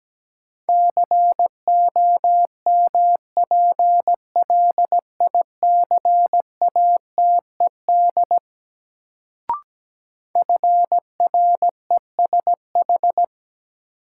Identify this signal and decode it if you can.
Morse code, heard as a keyed tone, about 17 wpm, 700 Hz. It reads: COMPLICATED FRESH
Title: Sentences from Top 1200 Words - New Word Drill - 17wpm